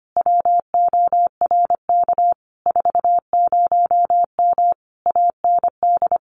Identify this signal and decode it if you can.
Morse code, heard as a keyed tone, about 25 wpm, 700 Hz. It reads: WORK 40M AND
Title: Common QSO Phrases Encore 25wpm